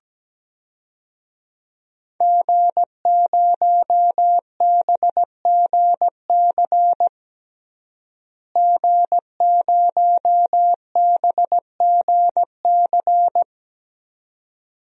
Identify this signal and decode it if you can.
Morse code, heard as a keyed tone, about 17 wpm, 700 Hz. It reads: G0BGC G0BGC
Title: DXCC-100 Call Signs - 5 Characters - Repeated 2x - 17wpm